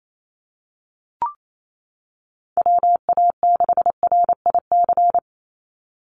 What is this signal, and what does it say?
Morse code, heard as a keyed tone, about 28 wpm, 700 Hz. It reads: WA6RIC
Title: NA SOTA Chaser Callsigns 28wpm